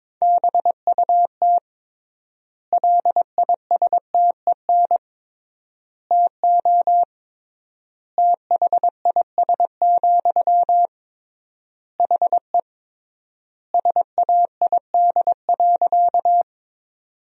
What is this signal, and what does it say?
Morse code, heard as a keyed tone, about 22 wpm, 700 Hz. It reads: BUT LISTEN TO THIS, HE SAID.
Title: 3x Word-Spacing - Sentences from Top 400 Words - 22wpm